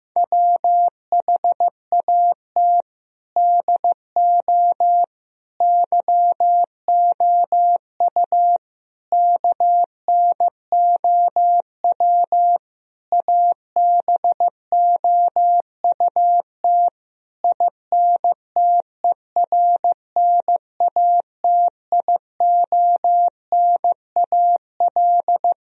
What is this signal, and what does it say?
Morse code, heard as a keyed tone, about 15 wpm, 700 Hz. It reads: WHAT DO YOU KNOW ABOUT INTERNATIONAL
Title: Sentences from Top 1000 Words - Review Part 4 - 15wpm